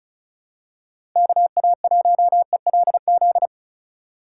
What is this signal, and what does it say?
Morse code, heard as a keyed tone, about 36 wpm, 700 Hz. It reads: KA1ELZ
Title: USA Call Signs - 6 Characters - 4x Character Spacing - 35wpm